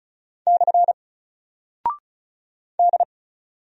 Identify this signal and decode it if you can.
Morse code, heard as a keyed tone, about 36 wpm, 700 Hz. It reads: / D
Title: Single Character (Letter:Number:Punctuation:Prosign) 35wpm